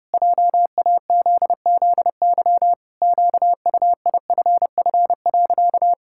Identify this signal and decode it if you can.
Morse code, heard as a keyed tone, about 30 wpm, 700 Hz. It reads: JAZZY QUIFF.
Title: Pangram Challenge v2 - 30wpm